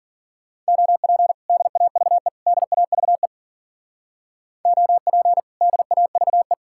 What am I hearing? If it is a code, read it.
Morse code, heard as a keyed tone, about 52 wpm, 700 Hz. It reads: OP DAVE DAVE OP DAVE